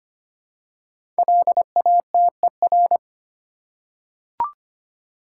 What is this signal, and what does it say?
Morse code, heard as a keyed tone, about 25 wpm, 700 Hz. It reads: LATER
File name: 5 Letter Words - First 3 Letters Given - 25wpm